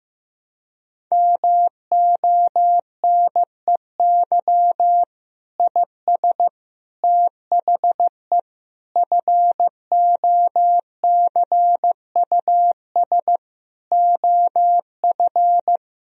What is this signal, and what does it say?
Morse code, heard as a keyed tone, about 16 wpm, 700 Hz. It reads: MONEY IS THE FOCUS OF
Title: Sentences from Top 1000 Words - Review Part 4 - 15wpm